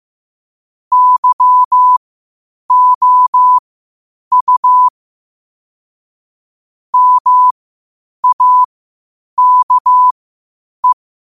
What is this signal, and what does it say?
Morse code, heard as a keyed tone, about 15 wpm, spaced as for 8 wpm, 1000 Hz. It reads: YOU MAKE
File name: qotdfiles / 8wpm